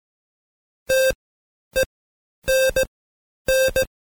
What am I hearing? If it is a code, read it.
Morse code, heard as a keyed tone, about 17 wpm, spaced as for 9 wpm, 530 Hz. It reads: TENN
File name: Code Quick / AudioDownld2 / Seg131-Stepper4 9WPM